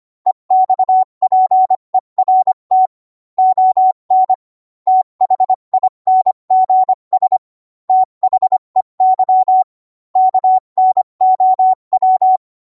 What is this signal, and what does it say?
Morse code, heard as a keyed tone, about 25 wpm, 750 Hz. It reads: EXPERT ON THINGS THEY KNOW